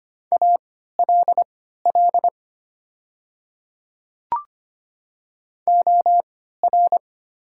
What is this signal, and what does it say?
Morse code, heard as a keyed tone, about 25 wpm, spaced as for 14 wpm, 700 Hz. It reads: ALL OR
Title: 3x Character Spacing - Top 100 Words - 25wpm